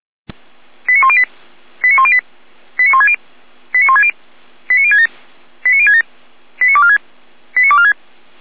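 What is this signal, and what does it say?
Five-tone selective call (ZVEI), heard as quick runs of five tones, tones of about 70 ms. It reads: 89108 89108 89160 89160 89067 89067 89246 89246
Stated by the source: ZVEI five-tone call 89150 received by SDR.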